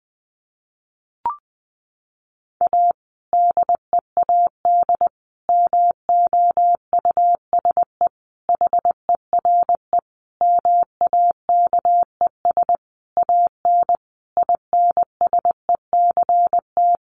Morse code, heard as a keyed tone, about 20 wpm, 700 Hz. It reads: A DEAD MOUSE HERE MAKES AN INSECT
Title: A Sound Of Thunder 20wpm